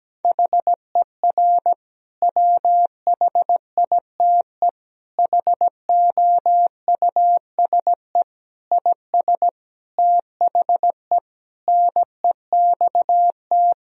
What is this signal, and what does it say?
Morse code, heard as a keyed tone, about 17 wpm, 700 Hz. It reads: HER WHITE HOUSE IS THE NEXT
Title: Sentences from Top 1000 Words - Review Part 1 - 17wpm